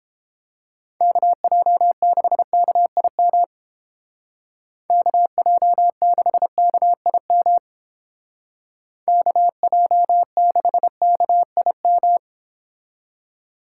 Morse code, heard as a keyed tone, about 29 wpm, 700 Hz. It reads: KJ6KIM KJ6KIM KJ6KIM